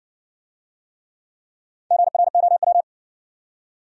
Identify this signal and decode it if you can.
Morse code, heard as a keyed tone, about 60 wpm, 700 Hz. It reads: GROW